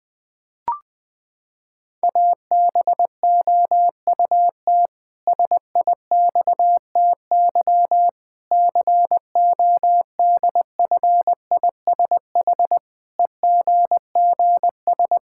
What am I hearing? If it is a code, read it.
Morse code, heard as a keyed tone, about 20 wpm, 700 Hz. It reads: ABOUT SIXTY CODFISH EGGS
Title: Pangram Challenge v2 - 20wpm